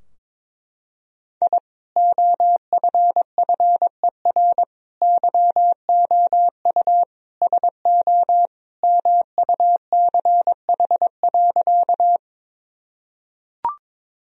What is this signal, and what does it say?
Morse code, heard as a keyed tone, about 22 wpm, 700 Hz. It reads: I OFFER YOU SO MUCH.
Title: Sentences from Top 1000 Words - Review Part 4 - 22wpm